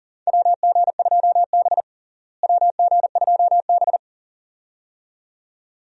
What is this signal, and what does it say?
Morse code, heard as a keyed tone, about 40 wpm, 700 Hz. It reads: WG2B WG2B
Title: USA Call Signs - 4 Characters - Repeated 2x - 40wpm